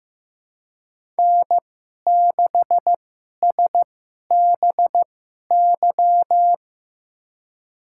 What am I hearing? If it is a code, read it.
Morse code, heard as a keyed tone, about 15 wpm, spaced as for 11 wpm, 700 Hz. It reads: N6SBY
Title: USA Call Signs - 2x Character Spacing - 15wpm